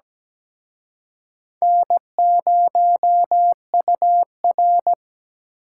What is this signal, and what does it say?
Morse code, heard as a keyed tone, about 20 wpm, 700 Hz. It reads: N0UR
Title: Speed Racing - USA Call Signs 17wpm